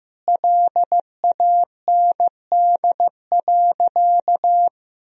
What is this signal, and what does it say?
Morse code, heard as a keyed tone, about 15 wpm, 700 Hz. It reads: LAND.
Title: Sentences from Top 1000 Words - Review Part 1 - 15wpm